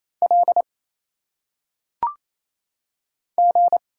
Morse code, heard as a keyed tone, about 27 wpm, 700 Hz. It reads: L G
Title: Single Letters - Phonetic Pronunciation 28wpm